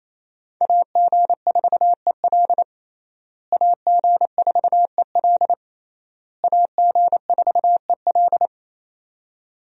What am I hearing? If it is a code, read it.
Morse code, heard as a keyed tone, about 28 wpm, 700 Hz. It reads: AG4EL AG4EL AG4EL
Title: USA Call Signs - 5 Characters - Repeated 3x - 28wpm